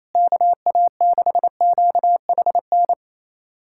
Morse code, heard as a keyed tone, about 27 wpm, 700 Hz. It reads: KA6QHN
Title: USA Callsign DE Callsign - 6 Characters - 28wpm